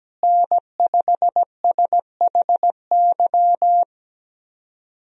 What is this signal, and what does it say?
Morse code, heard as a keyed tone, about 17 wpm, 700 Hz. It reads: N5SHY
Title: USA Call Signs - 5 Characters - Repeated 2x - 17wpm